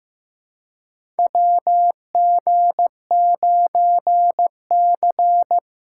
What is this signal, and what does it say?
Morse code, heard as a keyed tone, about 15 wpm, 700 Hz. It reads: WG9C
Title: USA Call Signs - 4 Characters - 4x Character Spacing - 15wpm